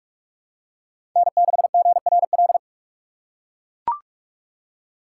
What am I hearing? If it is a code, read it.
Morse code, heard as a keyed tone, about 43 wpm, 700 Hz. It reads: N6GRL
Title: USA Call Signs - 3x Character Spacing - 45wpm